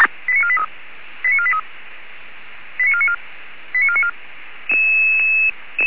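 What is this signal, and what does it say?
Five-tone selective call (ZVEI), heard as quick runs of five tones, tones of about 80 ms. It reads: 89483 89494 89494
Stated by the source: ZVEI five-tone selective call received on an SDR.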